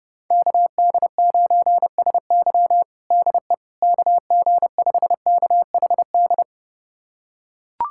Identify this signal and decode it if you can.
Morse code, heard as a keyed tone, about 31 wpm, 700 Hz. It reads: KD9SY DE KG5KHD
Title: USA Callsign DE Callsign 30wpm